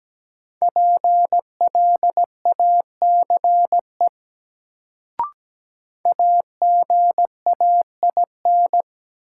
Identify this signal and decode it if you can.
Morse code, heard as a keyed tone, about 17 wpm, 700 Hz. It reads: PLACE AGAIN
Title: Sets of 4 Words - New Word Drill - 17wpm